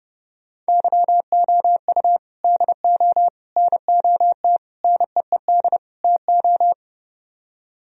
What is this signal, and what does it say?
Morse code, heard as a keyed tone, about 30 wpm, 700 Hz. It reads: YOU DO NOT NEED TO